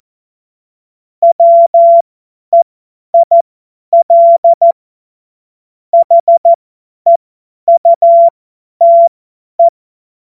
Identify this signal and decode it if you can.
Morse code, heard as a keyed tone, about 14 wpm, spaced as for 10 wpm, 680 Hz. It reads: WEIL HEUTE
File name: Morsekurs / Witze / Witz 10. Januar 2023 50 F70